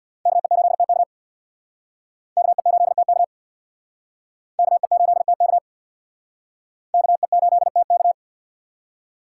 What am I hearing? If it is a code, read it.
Morse code, heard as a keyed tone, about 60 wpm, 700 Hz. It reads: KE8TK KE8TK KE8TK KE8TK
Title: Speed Racing - USA Call Signs 50wpm